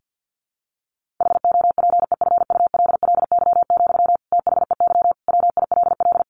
Morse code, heard as a keyed tone, about 50 wpm, 700 Hz. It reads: HOPEFULLY, THEY WILL
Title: Sentences from Top 1500 Words 50wpm